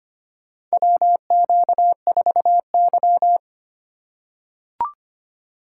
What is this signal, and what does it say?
Morse code, heard as a keyed tone, about 25 wpm, 700 Hz. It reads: WQ4Y
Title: USA Call Signs - 4 Characters - 2x Character Spacing - 25wpm